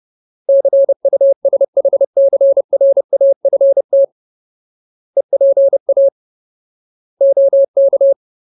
Morse code, heard as a keyed tone, about 30 wpm, 550 Hz. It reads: CUSHCRAFT EPA OK
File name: Ultimate Word List / 30WPM Ultimate